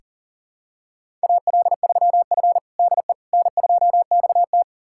Morse code, heard as a keyed tone, about 40 wpm, 700 Hz. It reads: AL3F DE N2XT
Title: USA Callsign DE Callsign - 4 Characters - 40wpm